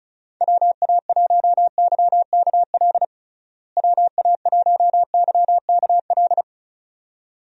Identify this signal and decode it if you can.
Morse code, heard as a keyed tone, about 36 wpm, 700 Hz. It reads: WA1YKL WA1YKL
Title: USA Call Signs - 6 Characters - Repeated 2x - 35wpm